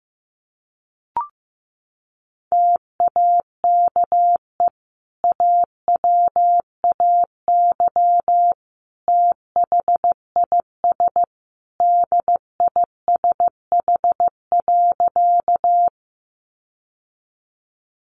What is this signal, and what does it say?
Morse code, heard as a keyed tone, about 15 wpm, 700 Hz. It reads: TAKE AWAY THIS DISH.
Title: Sentences from Top 1300 Words 15wpm